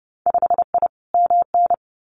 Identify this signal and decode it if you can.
Morse code, heard as a keyed tone, about 30 wpm, 700 Hz. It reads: 5I MN
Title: Winter Field Day Exchanges Encore 30wpm